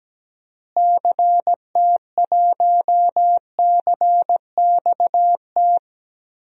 Morse code, heard as a keyed tone, about 17 wpm, 700 Hz. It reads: CT1CXT
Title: DXCC-100 Call Signs - 6 Characters - Repeated 3x - 17wpm